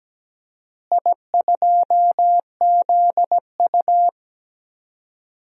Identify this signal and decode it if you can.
Morse code, heard as a keyed tone, about 17 wpm, 700 Hz. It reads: I2ZU
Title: DXCC-100 Call Signs - 4 Characters - Repeated 3x - 17wpm